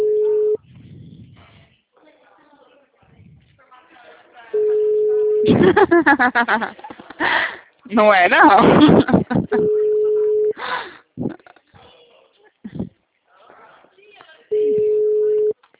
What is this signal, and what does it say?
Ringback tone (Europe, 425 Hz), with speech.